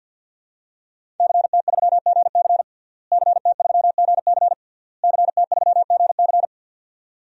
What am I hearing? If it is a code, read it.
Morse code, heard as a keyed tone, about 50 wpm, 700 Hz. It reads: CT3GC CT3GC CT3GC
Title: DXCC-100 Call Signs - 5 Characters - Repeated 3x - 50wpm